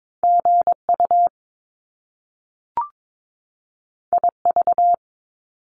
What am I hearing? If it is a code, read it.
Morse code, heard as a keyed tone, about 22 wpm, 700 Hz. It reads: GU IV
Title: Top Two Letter Combinations in Words 22wpm